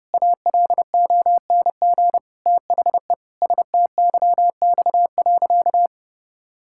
Morse code, heard as a keyed tone, about 30 wpm, 700 Hz. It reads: ALONG THE STYX.
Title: Pangram Challenge v2 - 30wpm